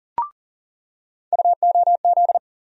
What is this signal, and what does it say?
Morse code, heard as a keyed tone, about 40 wpm, 700 Hz. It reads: UOZ
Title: USA Call Sign Postfixes - 3 Letters 40wpm